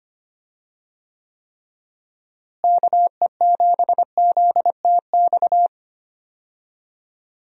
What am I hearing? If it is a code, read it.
Morse code, heard as a keyed tone, about 25 wpm, 700 Hz. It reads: KE7ZTX
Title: USA Call Signs - 6 Characters - Repeated 3x - 25wpm